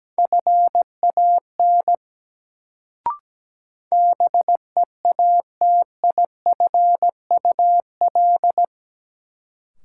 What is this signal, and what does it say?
Morse code, heard as a keyed tone, about 17 wpm, 700 Hz. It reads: FAN BEATIFUL
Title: Sentences from Top 1300 Words - New Word Drill Alt - 17wpm